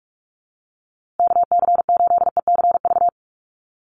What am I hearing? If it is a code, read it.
Morse code, heard as a keyed tone, about 44 wpm, 700 Hz. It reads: KC8ECV
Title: USA Call Signs Repeated 2x 45wpm